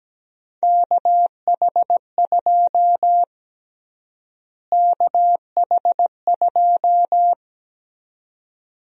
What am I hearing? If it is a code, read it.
Morse code, heard as a keyed tone, about 17 wpm, 700 Hz. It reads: KH2 KH2
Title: USA Call Sign Prefixes - Repeated 2x - 17wpm